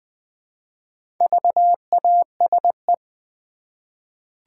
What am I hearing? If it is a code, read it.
Morse code, heard as a keyed tone, about 20 wpm, 700 Hz. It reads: VASE
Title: Easy Words with Hints - 20wpm